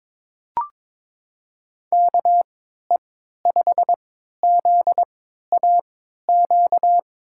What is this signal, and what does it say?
Morse code, heard as a keyed tone, about 22 wpm, spaced as for 12 wpm, 700 Hz. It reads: KE5ZAQ
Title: USA Call Signs - 6 Characters - 3x Character Spacing - 22wpm